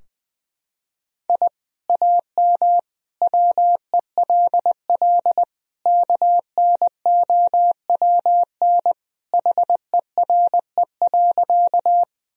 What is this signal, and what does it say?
Morse code, heard as a keyed tone, about 20 wpm, 700 Hz. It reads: I AM WELL KNOWN HERE.
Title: Sentences from Top 1000 Words - Review Part 3 - 20wpm